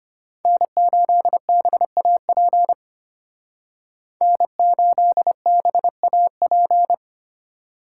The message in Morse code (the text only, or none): N8BAP N8BAP